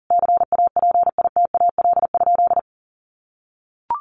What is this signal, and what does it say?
Morse code, heard as a keyed tone, about 40 wpm, 700 Hz. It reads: CAPITAL?
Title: Sentences from Top 1100 Words 40wpm